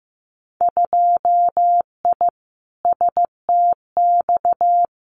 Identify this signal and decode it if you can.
Morse code, heard as a keyed tone, about 15 wpm, 700 Hz. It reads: 2I STX
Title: Winter Field Day Exchanges 15wpm